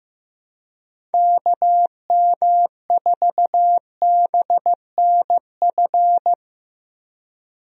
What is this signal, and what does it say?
Morse code, heard as a keyed tone, about 16 wpm, 700 Hz. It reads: KM4BNF
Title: USA Call Signs - 6 Characters - Repeated 3x - 15wpm